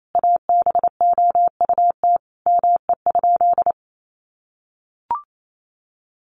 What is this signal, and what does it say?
Morse code, heard as a keyed tone, about 28 wpm, 700 Hz. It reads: ABOUT ME?
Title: Sentences from Top 200 Words 28wpm